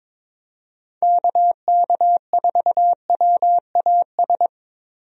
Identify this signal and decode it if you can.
Morse code, heard as a keyed tone, about 21 wpm, 700 Hz. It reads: KK4WAS
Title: USA Call Signs - 6 Characters - Repeated 3x - 22wpm